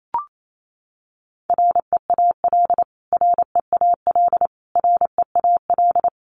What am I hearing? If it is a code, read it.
Morse code, heard as a keyed tone, about 28 wpm, 700 Hz. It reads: REAL REAL REAL
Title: Top 300 Words Repeated 28wpm